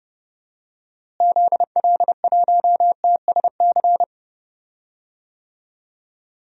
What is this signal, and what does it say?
Morse code, heard as a keyed tone, about 32 wpm, 700 Hz. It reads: ZL1TSC
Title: DXCC-100 Call Signs - 6 Characters - Repeated 3x - 30wpm